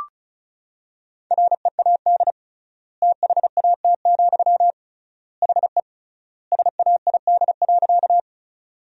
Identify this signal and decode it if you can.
Morse code, heard as a keyed tone, about 36 wpm, 700 Hz. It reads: READ THAT, HE SAID.
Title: 3x Word-Spacing - Sentences from Top 200 Words - 35wpm